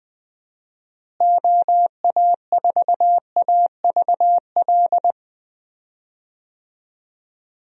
Morse code, heard as a keyed tone, about 20 wpm, 700 Hz. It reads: OA4AVL
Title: USA Call Signs - 6 Characters - Repeated 3x - 20wpm